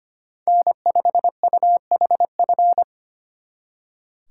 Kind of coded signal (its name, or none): Morse code